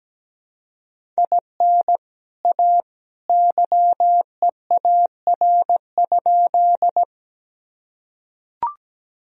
Morse code, heard as a keyed tone, about 17 wpm, 700 Hz. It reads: IN A YEAR?
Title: Sentences from Top 1000 Words - Review Part 1 - 17wpm